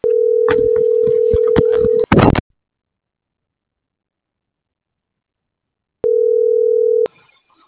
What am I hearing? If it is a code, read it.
Ringback tone (North America), 440 + 480 Hz.